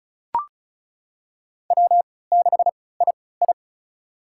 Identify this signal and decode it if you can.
Morse code, heard as a keyed tone, about 34 wpm, spaced as for 20 wpm, 700 Hz. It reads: W6II
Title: USA Call Signs - 3x Character Spacing - 35wpm